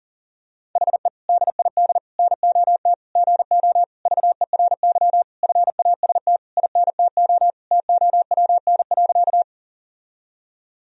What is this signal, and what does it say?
Morse code, heard as a keyed tone, about 37 wpm, 700 Hz. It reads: HE DID NOT GO VERY FAST INTO TOWN.
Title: Sentences from Top 1000 Words - Review Part 2 - 40wpm